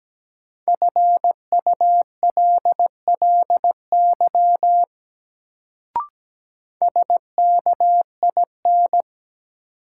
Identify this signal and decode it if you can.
Morse code, heard as a keyed tone, about 17 wpm, 700 Hz. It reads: FULLY SKIN
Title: Sentences from Top 1300 Words - New Word Drill - 17wpm